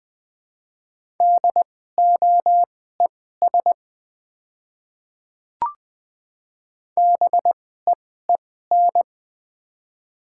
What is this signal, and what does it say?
Morse code, heard as a keyed tone, about 20 wpm, spaced as for 14 wpm, 700 Hz. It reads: DOES BEEN
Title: 2x Character Spacing - Top 300 Words - 20wpm